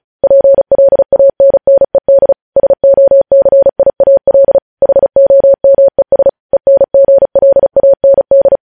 Morse code, heard as a keyed tone, about 36 wpm, 550 Hz. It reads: PLANNED SOCIAL HOMES ENGLAND